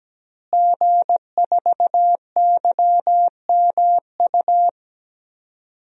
Morse code, heard as a keyed tone, about 17 wpm, 700 Hz. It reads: G4YMU
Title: DXCC-100 Call Signs - 5 Characters - Repeated 2x - 17wpm